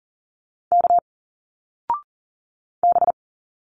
Morse code, heard as a keyed tone, about 40 wpm, 700 Hz. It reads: K B
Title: Single Letters 40wpm